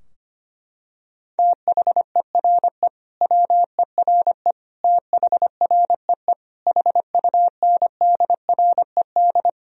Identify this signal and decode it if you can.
Morse code, heard as a keyed tone, about 25 wpm, 700 Hz. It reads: THERE WERE THREE HUNDRED
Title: Sentences from Top 1000 Words - Review Part 2 - 25wpm